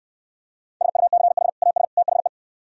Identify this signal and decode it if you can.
Morse code, heard as a keyed tone, about 72 wpm, 700 Hz. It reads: SUCH AS THE